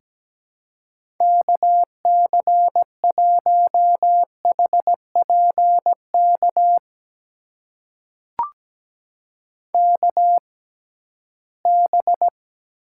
Morse code, heard as a keyed tone, about 17 wpm, spaced as for 13 wpm, 700 Hz. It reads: KC1HPK KB
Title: USA Call Signs - 6 Characters - 6x Character Spacing - 17wpm